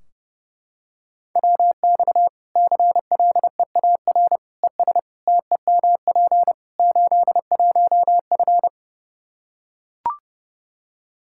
Morse code, heard as a keyed tone, about 32 wpm, 700 Hz. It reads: WX CLEAR ES TEMP 81F
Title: Practice QSO - Exchange 2 - Info - 30wpm